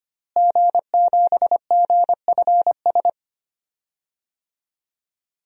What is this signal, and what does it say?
Morse code, heard as a keyed tone, about 25 wpm, 700 Hz. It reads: G7GFS
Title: DXCC-100 Call Signs - 5 Characters - Repeated 2x - 25wpm